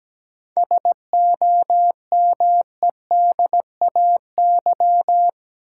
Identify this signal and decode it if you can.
Morse code, heard as a keyed tone, about 17 wpm, 700 Hz. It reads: SOMEDAY
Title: Sentences from Top 1600 Words - New Word Drill Alt - 17wpm